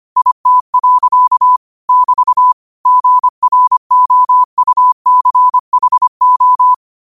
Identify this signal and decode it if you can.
Morse code, heard as a keyed tone, about 25 wpm, 1000 Hz. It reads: IT. <BT> GROUCHO
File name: qotdfiles / 25wpm